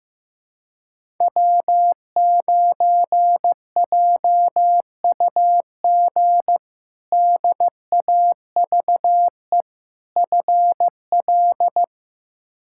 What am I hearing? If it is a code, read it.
Morse code, heard as a keyed tone, about 15 wpm, 700 Hz. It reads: W9JUG DAVE FL
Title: K1USN SST QSO Practice - 15wpm